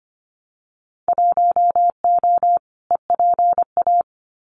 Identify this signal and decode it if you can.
Morse code, heard as a keyed tone, about 25 wpm, 700 Hz. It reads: 1O EPA
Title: Winter Field Day Exchanges 25wpm